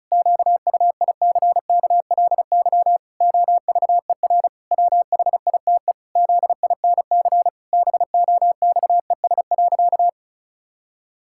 Morse code, heard as a keyed tone, about 34 wpm, 700 Hz. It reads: QUICKLY OVER WHITE ZINC BOXES.
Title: Pangram Challenge v2 - 35wpm